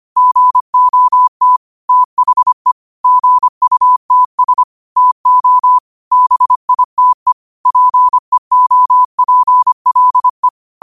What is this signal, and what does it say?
Morse code, heard as a keyed tone, about 25 wpm, 1000 Hz. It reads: GOT THE GUTS TO BITE PEOPLE
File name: qotdfiles / 25wpm